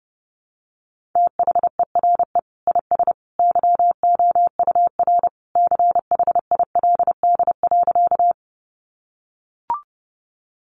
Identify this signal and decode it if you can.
Morse code, heard as a keyed tone, about 30 wpm, 700 Hz. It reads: THERE IS YOUR CHILD.
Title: Sentences from Top 800 Words 30wpm